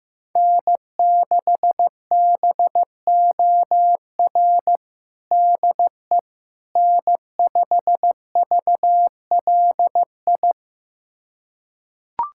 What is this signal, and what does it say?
Morse code, heard as a keyed tone, about 16 wpm, 700 Hz. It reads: N6BOR DE N5VLI
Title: USA Callsign DE Callsign - 5 Characters - 15wpm